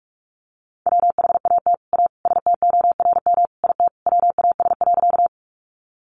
Morse code, heard as a keyed tone, about 44 wpm, 700 Hz. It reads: WHAT A STORM IT WAS.
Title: Sentences from Top 1500 Words 45wpm